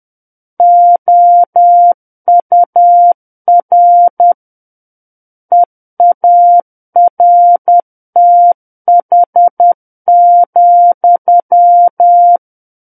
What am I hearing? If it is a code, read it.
Morse code, heard as a keyed tone, about 10 wpm, 700 Hz. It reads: OUR EARTH,
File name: Book One / 01-Chapter 1 - 10 WPM